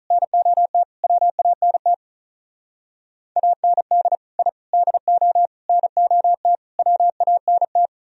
Morse code, heard as a keyed tone, about 44 wpm, 700 Hz. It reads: NOT WANT AND I DO NOT WANT
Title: Speed-Racing - Sets of 5 Words 35wpm